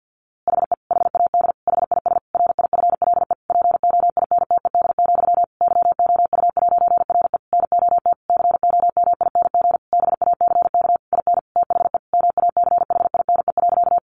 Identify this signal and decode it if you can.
Morse code, heard as a keyed tone, about 50 wpm, 700 Hz. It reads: HE HAD HIS RIFLE POINTED, YOU'RE NOT COMING BACK IN THE MACHINE.
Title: A Sound Of Thunder 50wpm